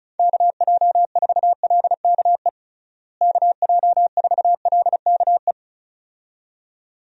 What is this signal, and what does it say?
Morse code, heard as a keyed tone, about 36 wpm, 700 Hz. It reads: KJ4LKE KJ4LKE
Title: USA Call Signs - 6 Characters - Repeated 2x - 35wpm